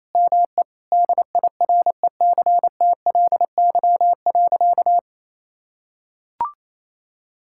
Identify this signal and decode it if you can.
Morse code, heard as a keyed tone, about 28 wpm, 700 Hz. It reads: ME DIRECTLY.
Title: Sentences from Top 1000 Words - Review Part 5 - 28wpm